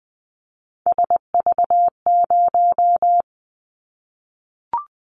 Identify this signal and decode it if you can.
Morse code, heard as a keyed tone, about 20 wpm, 700 Hz. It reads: SV0
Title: Callsign Prefixes for Top 50 DXCC - 20wpm